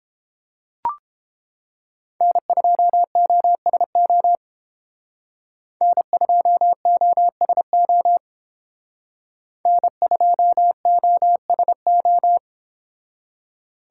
Morse code, heard as a keyed tone, about 30 wpm, 700 Hz. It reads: N2OSO N2OSO N2OSO